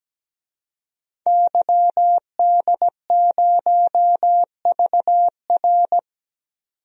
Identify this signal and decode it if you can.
Morse code, heard as a keyed tone, about 17 wpm, 700 Hz. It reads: YD0VR
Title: DXCC-100 Call Signs - 5 Characters - 17wpm